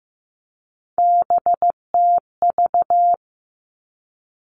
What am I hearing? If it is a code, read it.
Morse code, heard as a keyed tone, about 16 wpm, 700 Hz. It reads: BTV
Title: Three Challenging Letter-Number Combinations 15wpm